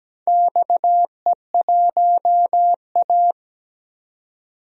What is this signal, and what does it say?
Morse code, heard as a keyed tone, about 17 wpm, 700 Hz. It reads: XE1A
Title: DXCC-100 Call Signs - 4 Characters - 17wpm